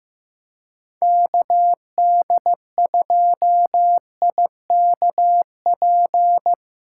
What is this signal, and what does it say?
Morse code, heard as a keyed tone, about 15 wpm, 700 Hz. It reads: KD2IKP
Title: USA Callsign DE Callsign - 6 Characters - 15wpm